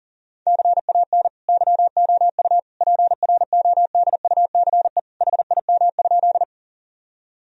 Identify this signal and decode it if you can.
Morse code, heard as a keyed tone, about 40 wpm, 700 Hz. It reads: CAN YOU PRODUCE HIM?
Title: Sentences from Top 1000 Words - Review Part 3 - 40wpm